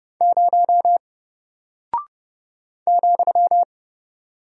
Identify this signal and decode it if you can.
Morse code, heard as a keyed tone, about 30 wpm, 700 Hz. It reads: 0 ,
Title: Single Character (Letter:Number:Punctuation:Prosign) Alt 30wpm